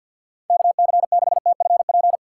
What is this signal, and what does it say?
Morse code, heard as a keyed tone, about 52 wpm, 700 Hz. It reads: KC6TFP
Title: USA Call Signs - 6 Characters - 4x Character Spacing - 50wpm